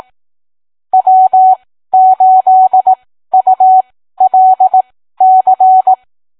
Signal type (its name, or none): Morse code